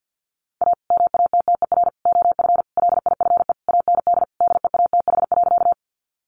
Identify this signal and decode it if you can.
Morse code, heard as a keyed tone, about 50 wpm, 700 Hz. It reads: A MATTER OF LIFE AND DEATH.
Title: Sentences from Top 1100 Words 50wpm